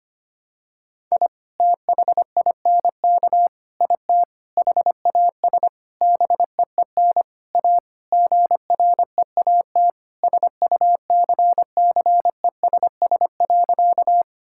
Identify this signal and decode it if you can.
Morse code, heard as a keyed tone, about 25 wpm, 700 Hz. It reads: I THINK IT HAS BEEN A GREAT SUCCESS.
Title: Sentences from Top 1000 Words - Review Part 6 - 25wpm